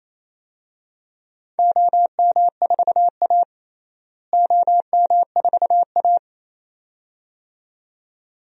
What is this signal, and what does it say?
Morse code, heard as a keyed tone, about 28 wpm, 700 Hz. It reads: OM4A OM4A
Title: DXCC-100 Call Signs - 4 Characters - Repeated 2x - 28wpm